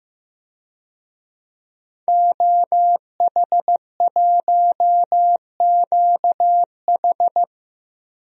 Morse code, heard as a keyed tone, about 15 wpm, 700 Hz. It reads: OH1QH
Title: DXCC-100 Call Signs - 5 Characters - Repeated 3x - 15wpm